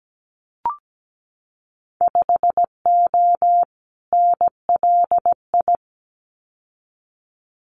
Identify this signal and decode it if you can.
Morse code, heard as a keyed tone, about 17 wpm, 700 Hz. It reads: 5O NLI
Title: Winter Field Day Exchanges 17wpm